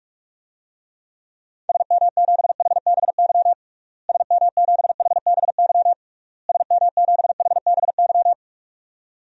Morse code, heard as a keyed tone, about 46 wpm, 700 Hz. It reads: SM7HBY SM7HBY SM7HBY
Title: DXCC-100 Call Signs - 6 Characters - Repeated 3x - 45wpm